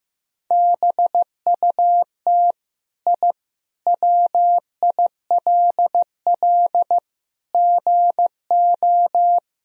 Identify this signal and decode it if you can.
Morse code, heard as a keyed tone, about 15 wpm, 700 Hz. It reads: BUT I WILL GO
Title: Sentences from Top 1000 Words - Review Part 3 - 15wpm